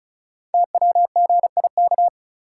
Morse code, heard as a keyed tone, about 36 wpm, 700 Hz. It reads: TWGIK
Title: Code Groups - 5 Letters - 35wpm